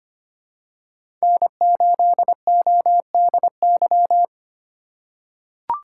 Morse code, heard as a keyed tone, about 25 wpm, 700 Hz. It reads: N8ODY
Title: USA Call Signs - 5 Characters - 3x Character Spacing - 25wpm